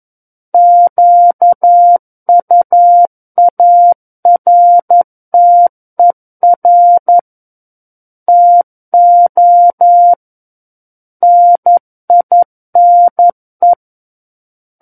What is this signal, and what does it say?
Morse code, heard as a keyed tone, about 11 wpm, 700 Hz. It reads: QUARTER TO NINE
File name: 11 WPM